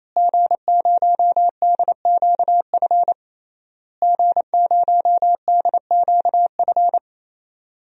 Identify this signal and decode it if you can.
Morse code, heard as a keyed tone, about 28 wpm, 700 Hz. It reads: G0DQF G0DQF
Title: DXCC-100 Call Signs - 5 Characters - Repeated 3x - 28wpm